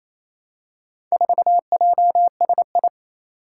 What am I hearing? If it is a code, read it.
Morse code, heard as a keyed tone, about 28 wpm, 700 Hz. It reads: 4JSI
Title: Four Challenging Letter-Number Combinations 28wpm